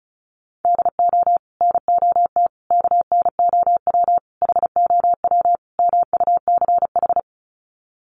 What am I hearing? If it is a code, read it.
Morse code, heard as a keyed tone, about 34 wpm, 700 Hz. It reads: DO NOT KNOW HOW MUCH
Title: Sets of 5 Words Encore 35wpm